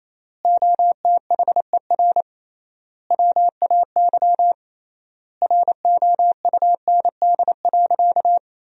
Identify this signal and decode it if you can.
Morse code, heard as a keyed tone, about 28 wpm, 700 Hz. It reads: OTHER WAY ROUND.
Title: 3x Word-Spacing - Sentences from Top 200 Words - 28wpm